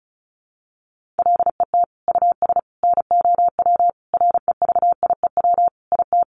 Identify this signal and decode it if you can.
Morse code, heard as a keyed tone, about 34 wpm, 700 Hz. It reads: LET US NOW REVIEW IT
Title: Sentences from Top 1500 Words 35wpm